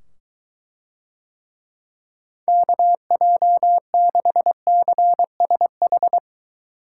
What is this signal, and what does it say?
Morse code, heard as a keyed tone, about 23 wpm, 700 Hz. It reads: KJ6CSH